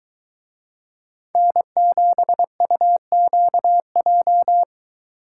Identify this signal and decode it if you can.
Morse code, heard as a keyed tone, about 23 wpm, 700 Hz. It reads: N7UQJ